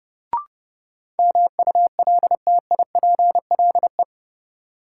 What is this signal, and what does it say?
Morse code, heard as a keyed tone, about 30 wpm, 700 Hz. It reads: MULTIPLE
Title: Sets of 5 Words - New Word Drill - 30wpm